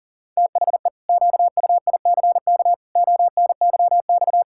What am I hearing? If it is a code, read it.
Morse code, heard as a keyed tone, about 40 wpm, 700 Hz. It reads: THE QUICK ONYX
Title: Pangram Challenge v2 - 40wpm